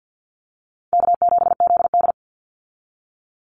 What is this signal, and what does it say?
Morse code, heard as a keyed tone, about 50 wpm, 700 Hz. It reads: K7ZD
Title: Easy Call Signs Repeated 2x - 50wpm